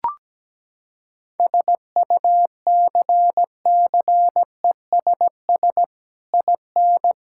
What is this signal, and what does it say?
Morse code, heard as a keyed tone, about 17 wpm, 700 Hz. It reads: SUCCESS IN
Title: Sentences from Top 1000 Words - Review Part 5 - 17wpm